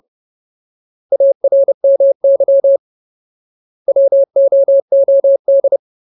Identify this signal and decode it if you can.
Morse code, heard as a keyed tone, about 30 wpm, 550 Hz. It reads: ARMY WOOD
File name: Ultimate Word List / 30WPM Ultimate